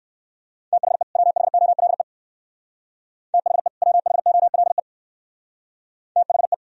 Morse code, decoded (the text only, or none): THE WHOLE THE WHOLE THE